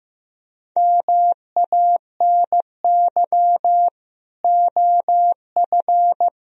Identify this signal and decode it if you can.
Morse code, heard as a keyed tone, about 18 wpm, 700 Hz. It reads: MANY OF
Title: Speed-Racing - Sets of 2 Words 15wpm